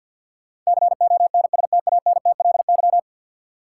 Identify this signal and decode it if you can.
Morse code, heard as a keyed tone, about 52 wpm, 700 Hz. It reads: CONSTANTLY
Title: Sentences from Top 1000 Words - Review Part 7 - New Word Drill - 50wpm